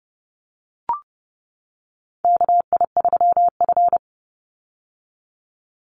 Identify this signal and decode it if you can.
Morse code, heard as a keyed tone, about 30 wpm, 700 Hz. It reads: KI3F
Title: Call Signs - 30wpm